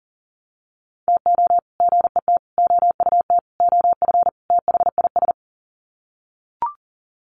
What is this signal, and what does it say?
Morse code, heard as a keyed tone, about 37 wpm, 700 Hz. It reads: TO GET OUT OF THIS
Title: Sets of 5 Words Encore2 40wpm